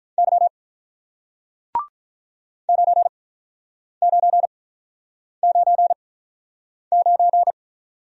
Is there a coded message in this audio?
Yes: Morse code from 0.2 to 7.5 s, heard as a keyed tone, about 48 wpm, 700 Hz. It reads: Y 9 9 9 9